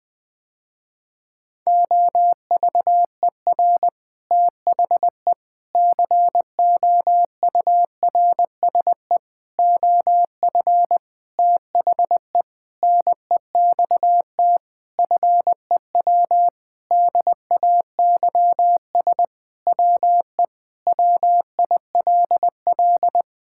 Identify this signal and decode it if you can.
Morse code, heard as a keyed tone, about 20 wpm, 700 Hz. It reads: OVER THE COURSE OF THE NEXT FEW DAYS WE WILL
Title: Sentences with 7-Word Phrases - Spoken Beginning - 20wpm